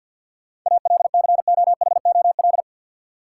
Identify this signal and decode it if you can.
Morse code, heard as a keyed tone, about 53 wpm, 700 Hz. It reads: ALCOHOL
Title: Sentences from Top 1400 Words - New Word Drill - 50wpm